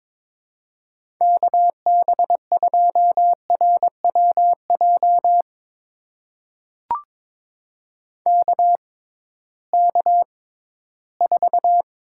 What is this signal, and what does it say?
Morse code, heard as a keyed tone, about 22 wpm, spaced as for 14 wpm, 700 Hz. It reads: KB2RWJ KK4
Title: USA Call Signs - 6x Character Spacing - 22wpm